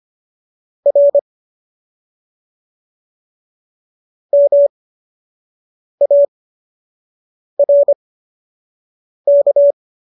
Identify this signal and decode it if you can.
Morse code, heard as a keyed tone, about 25 wpm, spaced as for 6 wpm, 570 Hz. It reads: R MARK